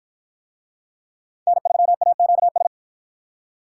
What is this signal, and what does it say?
Morse code, heard as a keyed tone, about 53 wpm, 700 Hz. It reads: N3AQS